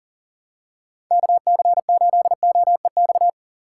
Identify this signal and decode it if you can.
Morse code, heard as a keyed tone, about 38 wpm, 700 Hz. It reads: KC8OEX
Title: USA Call Signs - 6 Characters - 6x Character Spacing - 40wpm